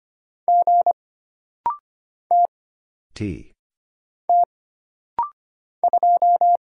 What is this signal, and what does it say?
Morse code, heard as a keyed tone, about 25 wpm, 700 Hz. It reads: G T T 2
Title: Single Letter-Number Alt - Rapid-Fire 25wpm